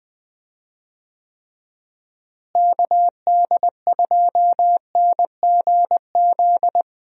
Morse code, heard as a keyed tone, about 20 wpm, 700 Hz. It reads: KD2NGZ
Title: USA Call Signs - 6 Characters - Repeated 2x - 20wpm